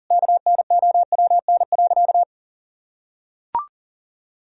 Morse code, heard as a keyed tone, about 40 wpm, 700 Hz. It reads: KNOWN.
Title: Sentences from Top 1000 Words - Review Part 3 - 40wpm